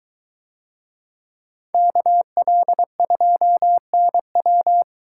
Morse code, heard as a keyed tone, about 23 wpm, 700 Hz. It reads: KL2NW